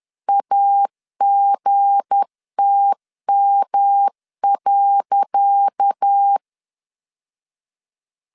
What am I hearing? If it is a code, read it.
Morse code, heard as a keyed tone, about 10 wpm, 800 Hz. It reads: AGTM.
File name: LEZIONI / 10 SIGNS / 10C DOT 10wpm 2940